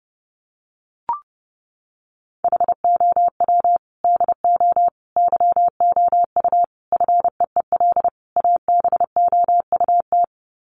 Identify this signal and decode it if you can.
Morse code, heard as a keyed tone, about 30 wpm, 700 Hz. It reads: HOW DO YOU FEEL ABOUT
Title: Sets of 5 Words 30wpm